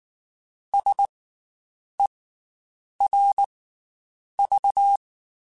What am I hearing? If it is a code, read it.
Morse code, heard as a keyed tone, about 19 wpm, 800 Hz. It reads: S E R V
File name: Special INGs / Special INGs-mp3s / Slow ing @18